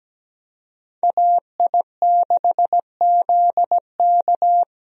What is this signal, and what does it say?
Morse code, heard as a keyed tone, about 17 wpm, 700 Hz. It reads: AI6ZK
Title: USA Call Signs - 5 Characters - Repeated 3x - 17wpm